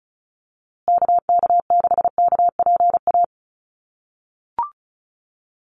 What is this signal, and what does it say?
Morse code, heard as a keyed tone, about 34 wpm, 700 Hz. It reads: KK6KPA
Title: USA Call Signs Repeated 3x 35wpm